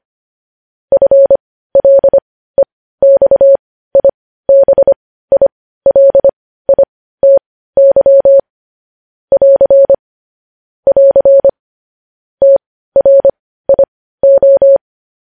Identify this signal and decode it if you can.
Morse code, heard as a keyed tone, about 25 wpm, spaced as for 15 wpm, 550 Hz. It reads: FLEXIBILITY <AR> <AR> TRIO
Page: news-headlines-in-morse-code-at-15-wpm